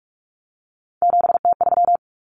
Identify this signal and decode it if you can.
Morse code, heard as a keyed tone, about 44 wpm, 700 Hz. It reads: 7T3
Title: Common Cut Numbers - 3-Digit 45wpm